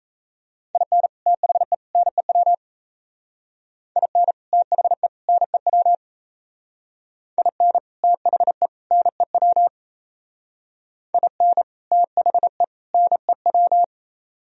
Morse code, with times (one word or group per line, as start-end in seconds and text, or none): 0.7-1.1 IN
1.3-1.7 THE
1.9-2.5 NEW
4.0-4.3 IN
4.5-5.1 THE
5.3-5.9 NEW
7.4-7.8 IN
8.0-8.7 THE
8.9-9.7 NEW
11.1-11.6 IN
11.9-12.6 THE
12.9-13.8 NEW